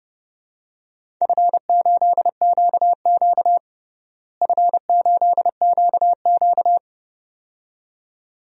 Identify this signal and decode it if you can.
Morse code, heard as a keyed tone, about 32 wpm, 700 Hz. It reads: F8QQ F8QQ
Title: DXCC-100 Call Signs - 4 Characters - Repeated 2x - 30wpm